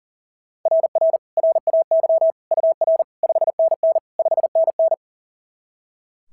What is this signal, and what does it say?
Morse code, heard as a keyed tone, about 42 wpm, 690 Hz. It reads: RR RAY UR 5NN 5NN
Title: SOTA QSO Practice 40wpm